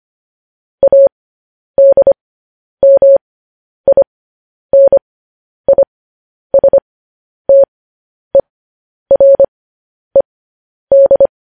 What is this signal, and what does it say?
Morse code, heard as a keyed tone, about 25 wpm, spaced as for 10 wpm, 550 Hz. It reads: ADMINISTERED
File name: podcast / News headlines-1x-25x10